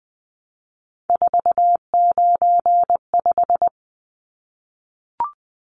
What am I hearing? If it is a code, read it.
Morse code, heard as a keyed tone, about 20 wpm, 700 Hz. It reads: 495
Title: Numbers - 3-Digit 20wpm